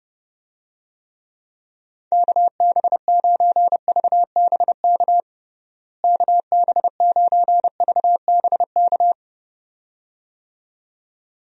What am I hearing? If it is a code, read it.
Morse code, heard as a keyed tone, about 30 wpm, 700 Hz. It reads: KB9VBK KB9VBK
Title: USA Call Signs - 6 Characters - Repeated 2x - 30wpm